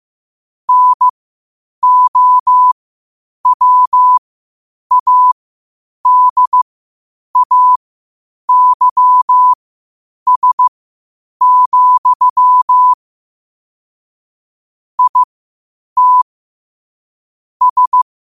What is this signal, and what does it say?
Morse code, heard as a keyed tone, about 15 wpm, spaced as for 8 wpm, 1000 Hz. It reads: NOWADAYS, IT S
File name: qotdfiles / 8wpm